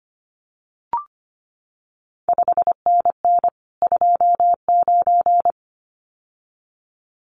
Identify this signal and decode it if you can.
Morse code, heard as a keyed tone, about 25 wpm, 700 Hz. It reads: 5NN 29
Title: CQ WW DX Contest Exchange Encore - 25wpm